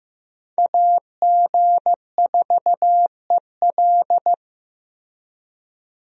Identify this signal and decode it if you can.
Morse code, heard as a keyed tone, about 15 wpm, 700 Hz. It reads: AG4EL
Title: USA Call Signs - 5 Characters - Repeated 3x - 15wpm